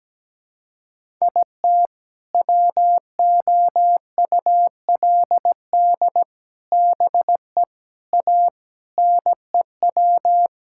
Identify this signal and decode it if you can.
Morse code, heard as a keyed tone, about 17 wpm, 700 Hz. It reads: IT WOULD BE A NEW
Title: Sentences from Top 1000 Words - Review Part 5 - 17wpm